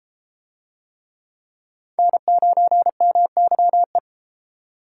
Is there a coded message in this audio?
Yes: Morse code from 2.0 to 4.0 s, heard as a keyed tone, about 31 wpm, 700 Hz. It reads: N9MYE